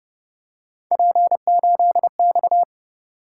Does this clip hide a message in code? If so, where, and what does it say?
Yes: Morse code from 0.9 to 2.6 s, heard as a keyed tone, about 29 wpm, 700 Hz. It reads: P8X